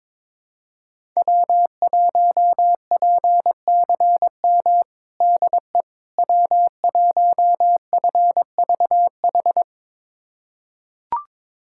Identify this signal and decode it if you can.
Morse code, heard as a keyed tone, about 21 wpm, 700 Hz. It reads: W1PCM DE W1FVH
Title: USA Callsign DE Callsign - 5 Characters - 22wpm